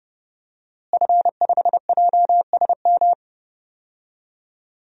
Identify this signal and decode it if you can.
Morse code, heard as a keyed tone, about 30 wpm, 700 Hz. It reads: F5JSM
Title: DXCC-100 Call Signs - 5 Characters - Repeated 3x - 30wpm